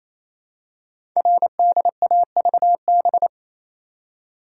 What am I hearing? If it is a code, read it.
Morse code, heard as a keyed tone, about 27 wpm, 700 Hz. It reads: RDAVB
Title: Code Groups - 5 Letters - 28wpm